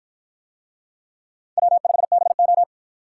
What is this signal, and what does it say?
Morse code, heard as a keyed tone, about 53 wpm, 700 Hz. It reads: W5BO